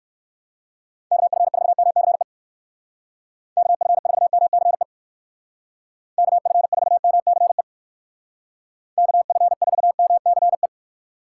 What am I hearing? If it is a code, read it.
Morse code, heard as a keyed tone, about 54 wpm, 700 Hz. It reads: KF4MCE KF4MCE KF4MCE KF4MCE
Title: Speed Racing - USA 2x3 Call Signs 45wpm